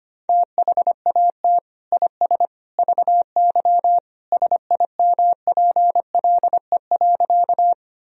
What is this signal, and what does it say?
Morse code, heard as a keyed tone, about 25 wpm, 700 Hz. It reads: THAT IS VY SIMPLE.
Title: Sentences from Top 400 Words and Abbreviations 25wpm